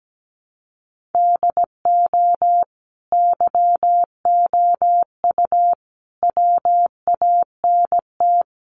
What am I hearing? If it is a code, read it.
Morse code, heard as a keyed tone, about 17 wpm, 700 Hz. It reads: DO YOU WANT
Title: Sentences from Top 1400 Words 17wpm